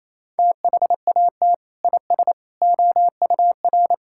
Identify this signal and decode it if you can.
Morse code, heard as a keyed tone, about 28 wpm, 700 Hz. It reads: THAT IS OUR
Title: Sentences from Top 1000 Words - Review Part 6 - 28wpm